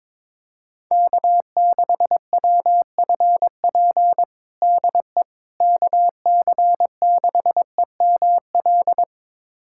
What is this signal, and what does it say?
Morse code, heard as a keyed tone, about 22 wpm, 700 Hz. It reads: K6WFP DE KC6EML
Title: USA Callsign DE Callsign 22wpm